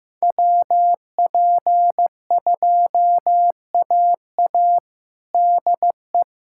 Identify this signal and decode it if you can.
Morse code, heard as a keyed tone, about 15 wpm, 700 Hz. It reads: WP2AA DE
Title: USA Callsign DE Callsign - 5 Characters - 15wpm